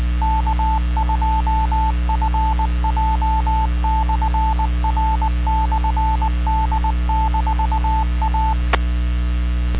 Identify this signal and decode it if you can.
Morse code, heard as a keyed tone, about 20 wpm, 880 Hz, with radio noise. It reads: K2FJ/R/D-A